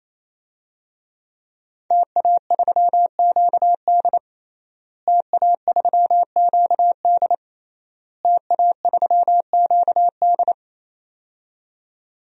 Morse code, heard as a keyed tone, about 27 wpm, 700 Hz. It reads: TA3QD TA3QD TA3QD
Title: DXCC-100 Call Signs - 5 Characters - Repeated 3x - 28wpm